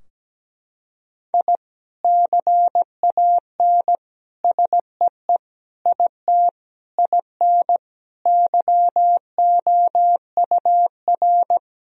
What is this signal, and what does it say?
Morse code, heard as a keyed tone, about 17 wpm, 700 Hz. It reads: I CAN SEE IT IN YOUR
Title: Sentences from Top 1000 Words - Review Part 2 - 17wpm